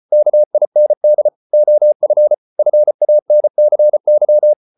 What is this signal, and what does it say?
Morse code, heard as a keyed tone, about 34 wpm, 600 Hz. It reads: KIND OF FANCY